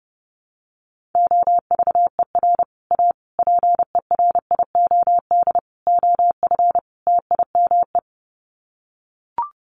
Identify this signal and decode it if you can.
Morse code, heard as a keyed tone, about 30 wpm, 700 Hz. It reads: OVER A PERIOD OF TIME
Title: Sets of 5 Words Encore2 30wpm